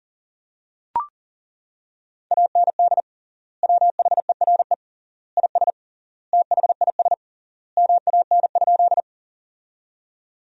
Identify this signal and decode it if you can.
Morse code, heard as a keyed tone, about 40 wpm, 700 Hz. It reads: AND WHERE IS THIS MAN?
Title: 3x Word-Spacing - Sentences from Top 200 Words - 40wpm